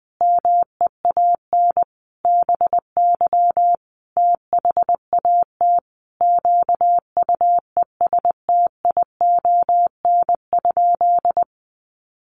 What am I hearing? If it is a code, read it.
Morse code, heard as a keyed tone, about 20 wpm, 700 Hz. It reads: MEAN BY THAT QUESTION?
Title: Sentences from Top 400 Words 20wpm